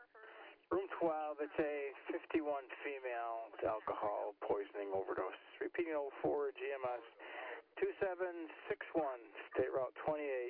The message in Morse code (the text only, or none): none